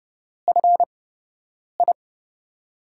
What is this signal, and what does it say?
Morse code, heard as a keyed tone, about 30 wpm, 700 Hz. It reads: F I
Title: USA Call Signs - 8x Character Spacing - 30wpm